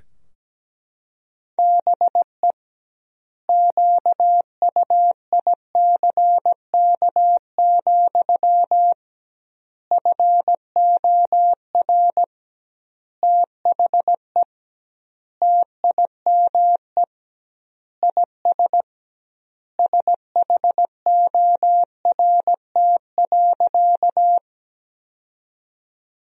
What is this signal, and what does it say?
Morse code, heard as a keyed tone, about 17 wpm, 700 Hz. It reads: BE QUICK, FOR THE TIME IS SHORT.
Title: 2x Word-Spacing - Sentences from Top 500 Words - 17wpm